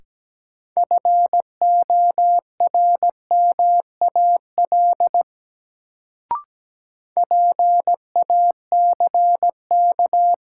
Morse code, heard as a keyed tone, about 17 wpm, 700 Hz. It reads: FORMAL PACK
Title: Sentences from Top 1300 Words - New Word Drill - 17wpm